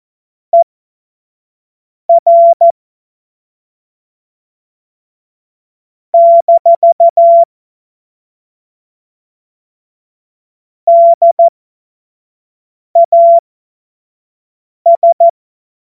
Morse code, heard as a keyed tone, about 14 wpm, spaced as for 5 wpm, 680 Hz. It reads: ER - DAS
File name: Morsekurs / Witze / Witz 11. Mai 2021, 25 F70